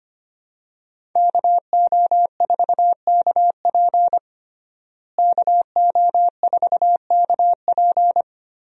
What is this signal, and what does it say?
Morse code, heard as a keyed tone, about 25 wpm, 700 Hz. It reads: KO4KP KO4KP
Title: USA Call Signs - 5 Characters - Repeated 2x - 25wpm